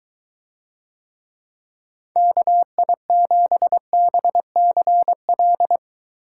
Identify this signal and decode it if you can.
Morse code, heard as a keyed tone, about 23 wpm, 700 Hz. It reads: KI7BCL